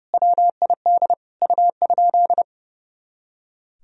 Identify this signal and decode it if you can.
Morse code, heard as a keyed tone, about 30 wpm, 700 Hz. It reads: WID U?
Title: Sentences from Top 100 Words and Abbreviations 30wpm